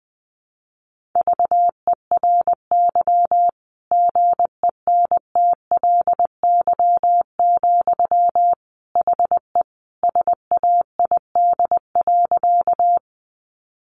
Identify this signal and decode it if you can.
Morse code, heard as a keyed tone, about 20 wpm, 700 Hz. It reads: VERY GENTLY, HE SAID.
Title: Sentences from Top 1500 Words 20wpm